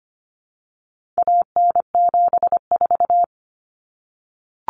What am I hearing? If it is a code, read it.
Morse code, heard as a keyed tone, about 25 wpm, 700 Hz. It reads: AN74
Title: Common Cut Numbers - 4-Digit 25wpm